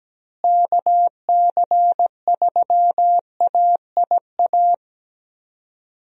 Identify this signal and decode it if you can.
Morse code, heard as a keyed tone, about 17 wpm, 700 Hz. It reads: KC3AIA
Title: USA Call Signs - 6 Characters - Repeated 3x - 17wpm